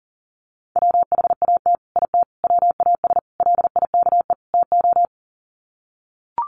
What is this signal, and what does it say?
Morse code, heard as a keyed tone, about 40 wpm, 700 Hz. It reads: WHAT IT WAS LIKE TO
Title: Sets of 5 Words Encore 40wpm